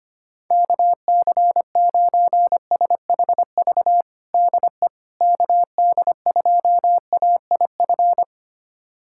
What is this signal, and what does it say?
Morse code, heard as a keyed tone, about 25 wpm, 700 Hz. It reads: KC9SHV DE KD2AIF
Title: USA Callsign DE Callsign - 6 Characters - 25wpm